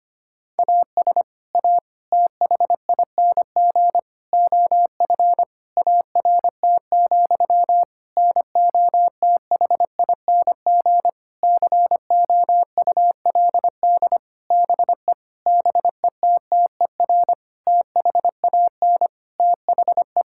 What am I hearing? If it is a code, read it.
Morse code, heard as a keyed tone, about 25 wpm, 700 Hz. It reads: AS A THING OF ART, NOTHING COULD BE BETTER THAN THE
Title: Sentences from Top 1000 Words - Review Part 6 - 25wpm